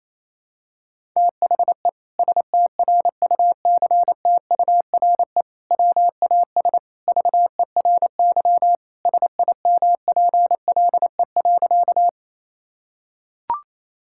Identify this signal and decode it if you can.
Morse code, heard as a keyed tone, about 28 wpm, 700 Hz. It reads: THE STRUCTURE WAS VERY SIMPLE.
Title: Sentences from Top 1000 Words - Review Part 6 - 28wpm